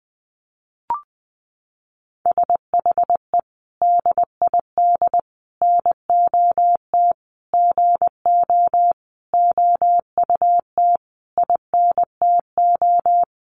Morse code, heard as a keyed tone, about 20 wpm, 700 Hz. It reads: SHE DID NOT GO OUT INTO
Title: Sentences from Top 700 Words 20wpm